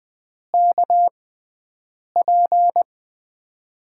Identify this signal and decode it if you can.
Morse code, heard as a keyed tone, about 20 wpm, spaced as for 7 wpm, 700 Hz. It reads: KP
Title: USA Call Signs - 6x Character Spacing - 20wpm